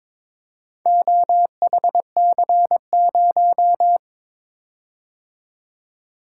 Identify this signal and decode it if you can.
Morse code, heard as a keyed tone, about 22 wpm, 700 Hz. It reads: OHC0
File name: practice / 22wpm / Four Letter-Number Combinations - 22wpm